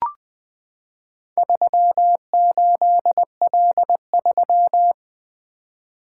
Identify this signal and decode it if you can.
Morse code, heard as a keyed tone, about 20 wpm, 700 Hz. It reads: 38L3
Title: Four Letter-Number Combinations - 20wpm